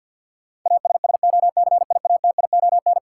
Morse code, heard as a keyed tone, about 46 wpm, 700 Hz. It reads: ASSOCIATION
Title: Sentences from Top 1100 Words - New Word Drill Alt - 50wpm